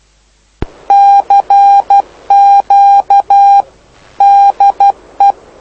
Morse code, heard as a keyed tone, about 12 wpm, 800 Hz. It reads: CQ DE
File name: lu4aao / lu7aa 09dic2011 exp planeador vuelo1